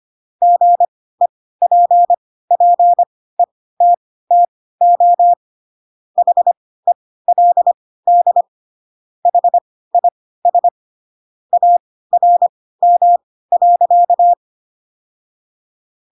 Morse code, heard as a keyed tone, about 25 wpm, spaced as for 16 wpm, 700 Hz. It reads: GEPPETTO HELD HIS ARM.
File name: Pinocchio / Pinocchio ch7 10 25 ik1hge